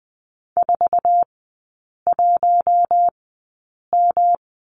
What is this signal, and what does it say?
Morse code, heard as a keyed tone, about 20 wpm, spaced as for 8 wpm, 700 Hz. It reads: 41M
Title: 114 Lesson 19e - Letter w - Three Characters 20wpm